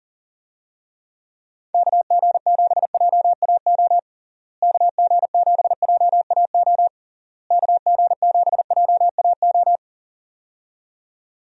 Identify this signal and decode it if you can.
Morse code, heard as a keyed tone, about 40 wpm, 700 Hz. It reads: KG7JAO KG7JAO KG7JAO
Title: USA Call Signs - 6 Characters - Repeated 3x - 40wpm